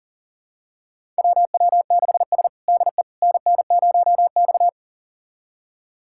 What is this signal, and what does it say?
Morse code, heard as a keyed tone, about 37 wpm, 700 Hz. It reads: WW6S DE NN0X
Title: USA Callsign DE Callsign - 4 Characters - 40wpm